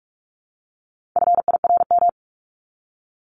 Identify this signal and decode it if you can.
Morse code, heard as a keyed tone, about 44 wpm, 700 Hz. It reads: FIRM
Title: Top 1251-1500 Words 45wpm